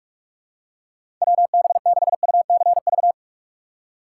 Morse code, heard as a keyed tone, about 41 wpm, 700 Hz. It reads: WB6UCV
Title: USA Call Signs - 2x Character Spacing - 45wpm